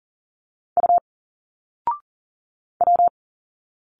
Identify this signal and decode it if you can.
Morse code, heard as a keyed tone, about 40 wpm, 700 Hz. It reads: U W
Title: Single Letters 40wpm